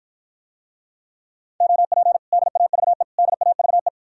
Morse code, heard as a keyed tone, about 54 wpm, 700 Hz. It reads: OP DAVE DAVE